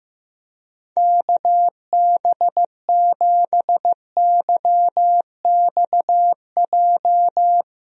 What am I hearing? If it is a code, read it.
Morse code, heard as a keyed tone, about 15 wpm, 700 Hz. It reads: KB7YXJ
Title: USA Call Signs - 6 Characters - Repeated 2x - 15wpm